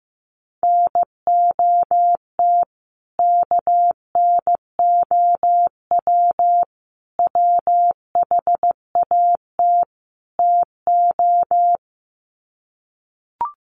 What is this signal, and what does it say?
Morse code, heard as a keyed tone, about 14 wpm, 700 Hz. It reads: NOT KNOW WHAT TO
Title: Sets of 4 Words 15wpm